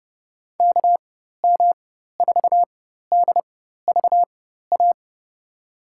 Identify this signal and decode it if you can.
Morse code, heard as a keyed tone, about 30 wpm, spaced as for 14 wpm, 700 Hz. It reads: KM4DVA
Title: USA Call Signs - 6 Characters - 4x Character Spacing - 30wpm